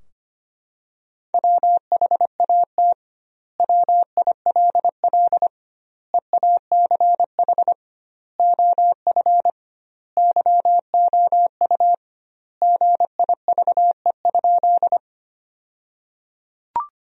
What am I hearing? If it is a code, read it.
Morse code, heard as a keyed tone, about 25 wpm, 700 Hz. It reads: WHAT WILL EACH OF YOU GIVE?
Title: 2x Word-Spacing - Sentences from Top 200 Words - 25wpm